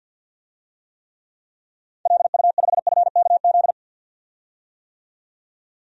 Morse code, heard as a keyed tone, about 51 wpm, 700 Hz. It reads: LU5FKZ
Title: DXCC-100 Call Signs - 6 Characters - 50wpm